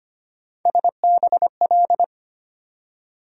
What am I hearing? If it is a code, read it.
Morse code, heard as a keyed tone, about 25 wpm, 700 Hz. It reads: SBL
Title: USA Call Sign Postfixes - 3 Letters 25wpm